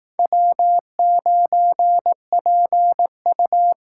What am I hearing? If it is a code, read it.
Morse code, heard as a keyed tone, about 19 wpm, 700 Hz. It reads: W9PU